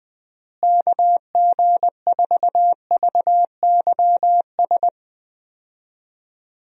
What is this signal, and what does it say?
Morse code, heard as a keyed tone, about 20 wpm, 700 Hz. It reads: KG4VYS
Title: USA Call Signs - 6 Characters - Repeated 2x - 20wpm